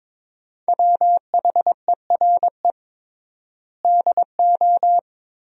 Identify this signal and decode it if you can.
Morse code, heard as a keyed tone, about 22 wpm, 700 Hz. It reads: WHERE DO
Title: 3x Word-Spacing - Sentences from Top 200 Words - 22wpm